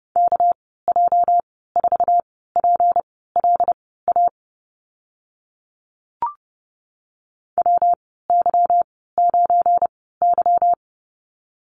Morse code, heard as a keyed tone, about 30 wpm, spaced as for 17 wpm, 700 Hz. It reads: KJ4PLA WY9Y
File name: Lesson 37g - Letter x - Call Signs 30wpm